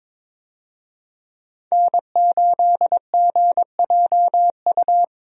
Morse code, heard as a keyed tone, about 22 wpm, 700 Hz. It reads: N8GJU
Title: USA Call Signs - 5 Characters - Repeated 2x - 22wpm